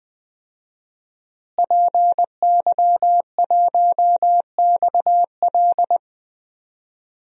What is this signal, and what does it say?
Morse code, heard as a keyed tone, about 21 wpm, 700 Hz. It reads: PY1XL
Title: DXCC-100 Call Signs - 5 Characters - Repeated 2x - 20wpm